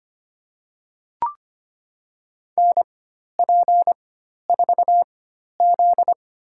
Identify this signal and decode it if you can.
Morse code, heard as a keyed tone, about 25 wpm, spaced as for 12 wpm, 700 Hz. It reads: NP4Z
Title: USA Call Signs - 4 Characters - 4x Character Spacing - 25wpm